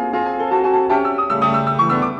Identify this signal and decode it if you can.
There is no code.